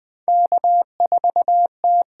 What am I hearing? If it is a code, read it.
Morse code, heard as a keyed tone, about 20 wpm, 700 Hz. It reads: K4T
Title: USA Call Signs - 8x Character Spacing - 20wpm